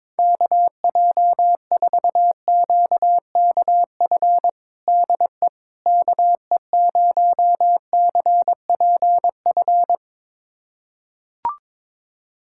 Morse code, heard as a keyed tone, about 22 wpm, 700 Hz. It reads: KJ4QKF DE KE0CPF
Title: USA Callsign DE Callsign - 6 Characters - 22wpm